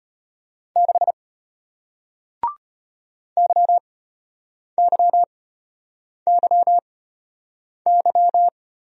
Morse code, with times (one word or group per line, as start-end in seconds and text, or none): 0.8-1.1 6
3.4-8.5 YYYY